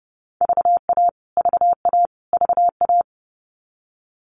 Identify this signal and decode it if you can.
Morse code, heard as a keyed tone, about 30 wpm, 700 Hz. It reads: VA VA VA
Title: US States Repeated 3x 30wpm